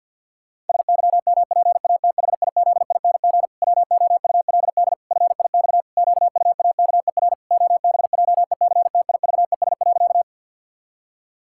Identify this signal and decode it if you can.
Morse code, heard as a keyed tone, about 52 wpm, 700 Hz. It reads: SYMPATHIZING WOULD FIX QUAKER OBJECTIVES.
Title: Pangram Challenge v2 - 50wpm